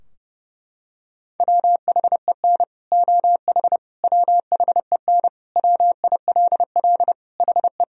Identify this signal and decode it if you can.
Morse code, heard as a keyed tone, about 30 wpm, 700 Hz. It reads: WHEN OH WHEN WILL HE
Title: Sentences from Top 1000 Words - Review Part 2 - 30wpm